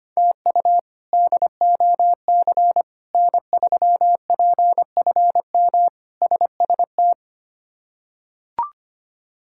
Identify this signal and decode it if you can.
Morse code, heard as a keyed tone, about 25 wpm, 700 Hz. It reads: TU DOC N3PFM SST
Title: K1USN SST QSO Practice - 25wpm